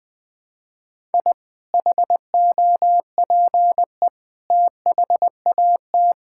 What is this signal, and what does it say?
Morse code, heard as a keyed tone, about 21 wpm, 700 Hz. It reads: I HOPE THAT
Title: Sentences with 2-Word Phrases - 20wpm